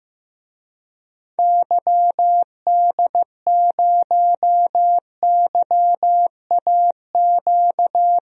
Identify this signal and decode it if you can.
Morse code, heard as a keyed tone, about 15 wpm, 700 Hz. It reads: YD0YAQ
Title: DXCC-100 Call Signs - 6 Characters - 15wpm